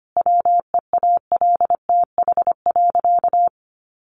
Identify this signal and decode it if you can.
Morse code, heard as a keyed tone, about 25 wpm, 700 Hz. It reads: WEALTH.
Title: Sentences from Top 1200 Words 25wpm